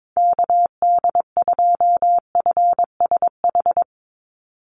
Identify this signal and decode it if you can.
Morse code, heard as a keyed tone, about 22 wpm, 700 Hz. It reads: KD2FSH
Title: Call Signs Repeated 2x - 22wpm